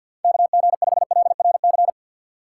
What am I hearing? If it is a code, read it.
Morse code, heard as a keyed tone, about 53 wpm, 700 Hz. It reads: KG5LRC
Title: USA Call Signs - 6 Characters - 3x Character Spacing - 50wpm